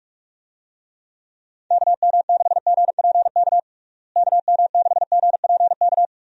Morse code, heard as a keyed tone, about 41 wpm, 700 Hz. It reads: KM6GPK KM6GPK
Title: USA Call Signs - 6 Characters - Repeated 2x - 45wpm